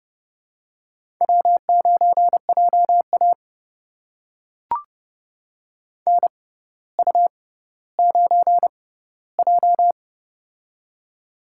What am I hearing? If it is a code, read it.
Morse code, heard as a keyed tone, about 30 wpm, spaced as for 15 wpm, 700 Hz. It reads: W9JA NU9J
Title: USA Call Signs - 4 Characters - 6x Character Spacing - 30wpm